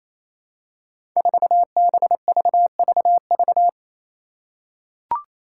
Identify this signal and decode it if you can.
Morse code, heard as a keyed tone, about 28 wpm, 700 Hz. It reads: 4BVVV
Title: Five Challenging Letter-Number Combinations 28wpm